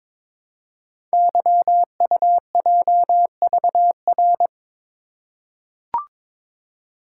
Morse code, heard as a keyed tone, about 22 wpm, 700 Hz. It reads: YUJVR
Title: Code Groups - 5 Letters - 22wpm